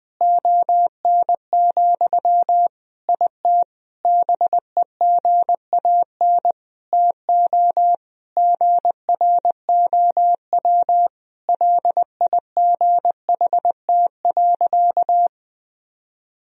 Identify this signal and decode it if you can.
Morse code, heard as a keyed tone, about 20 wpm, 700 Hz. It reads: ON, IT BEGAN TO GROW LIGHT.
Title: Sentences from Top 1000 Words - Review Part 1 - 20wpm